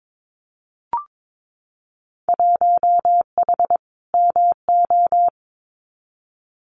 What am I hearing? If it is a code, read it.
Morse code, heard as a keyed tone, about 22 wpm, 700 Hz. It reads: 1H MO
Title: Winter Field Day Exchanges 22wpm